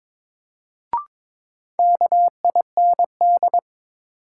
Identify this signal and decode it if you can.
Morse code, heard as a keyed tone, about 22 wpm, 700 Hz. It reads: KIND
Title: Sets of 2 Words - New Word Drill - 22wpm